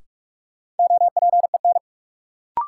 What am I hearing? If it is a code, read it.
Morse code, heard as a keyed tone, about 46 wpm, 700 Hz. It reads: OPEN
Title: Sentences from Top 300 Words - New Word Drill - 45wpm